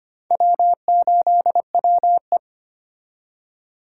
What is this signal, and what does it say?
Morse code, heard as a keyed tone, about 34 wpm, spaced as for 30 wpm, 700 Hz. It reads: W8WE
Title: Speed Racing - USA 1x2 Call Signs 25wpm